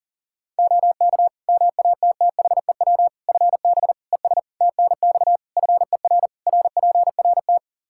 Morse code, heard as a keyed tone, about 40 wpm, 700 Hz. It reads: OK MATTHEW FB ES TNX FER RPRT
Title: Practice QSO - Exchange 2 - Info - 40wpm